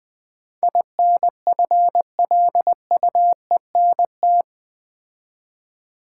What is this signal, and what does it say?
Morse code, heard as a keyed tone, about 20 wpm, 700 Hz. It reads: INFLUENT
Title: Build A Word - 4 Letters At A Time - 11-Letter Words - 20wpm